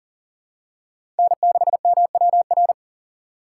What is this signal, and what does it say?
Morse code, heard as a keyed tone, about 40 wpm, 700 Hz. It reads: N6MWR
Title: USA Call Signs - 5 Characters - 3x Character Spacing - 40wpm